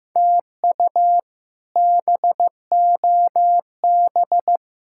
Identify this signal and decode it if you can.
Morse code, heard as a keyed tone, about 16 wpm, 700 Hz. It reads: TU BOB
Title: K1USN SST QSO Practice - 15wpm